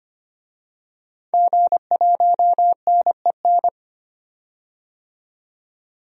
Morse code, heard as a keyed tone, about 25 wpm, 700 Hz. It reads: G1NEN
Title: DXCC-100 Call Signs - 5 Characters - 25wpm